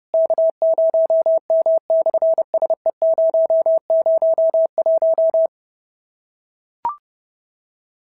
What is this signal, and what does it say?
Morse code, heard as a keyed tone, about 28 wpm, 650 Hz. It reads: K0M/SE001
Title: SOTA QSO Practice 30wpm